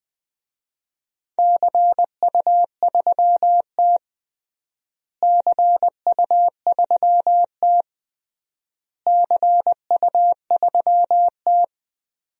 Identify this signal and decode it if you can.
Morse code, heard as a keyed tone, about 20 wpm, 700 Hz. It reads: CU3T CU3T CU3T
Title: DXCC-100 Call Signs - 4 Characters - Repeated 3x - 20wpm